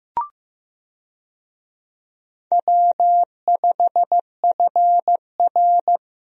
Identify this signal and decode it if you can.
Morse code, heard as a keyed tone, about 15 wpm, 700 Hz. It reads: W5FR
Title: USA Call Signs - 4 Characters - 15wpm